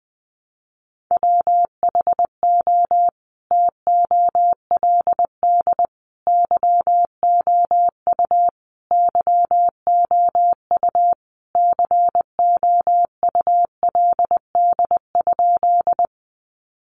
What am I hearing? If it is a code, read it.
Morse code, heard as a keyed tone, about 19 wpm, 700 Hz. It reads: WHO TOLD YOU YOU COULD?
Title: Sentences from Top 400 Words 20wpm